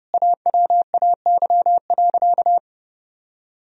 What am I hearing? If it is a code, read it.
Morse code, heard as a keyed tone, about 30 wpm, 700 Hz. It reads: AWAY.
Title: Sentences from Top 1000 Words - Review Part 2 - 30wpm